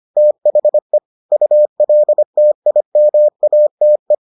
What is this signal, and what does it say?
Morse code, heard as a keyed tone, about 25 wpm, 600 Hz. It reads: THE ULTIMATE